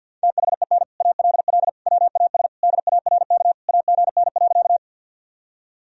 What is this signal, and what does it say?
Morse code, heard as a keyed tone, about 52 wpm, 700 Hz. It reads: THEN ALL WAS DARK AGN.
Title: Sentences from Top 500 Words and Abbreviations 50wpm